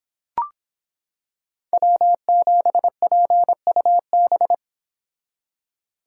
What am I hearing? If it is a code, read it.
Morse code, heard as a keyed tone, about 26 wpm, 700 Hz. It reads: W7PUB